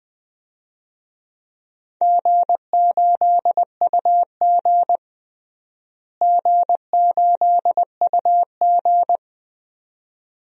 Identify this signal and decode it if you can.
Morse code, heard as a keyed tone, about 20 wpm, 700 Hz. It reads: G8UG G8UG
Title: DXCC-100 Call Signs - 4 Characters - Repeated 2x - 20wpm